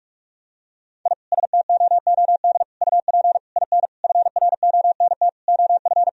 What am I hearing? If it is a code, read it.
Morse code, heard as a keyed tone, about 46 wpm, 700 Hz. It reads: I STOOD UP IN FRONT OF
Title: Sentences from Top 1000 Words - Review Part 2 - 45wpm